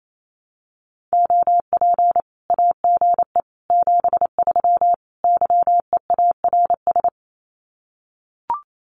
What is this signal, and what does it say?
Morse code, heard as a keyed tone, about 28 wpm, 700 Hz. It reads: OP AGE 73 YEARS
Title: Common QSO Phrases 28wpm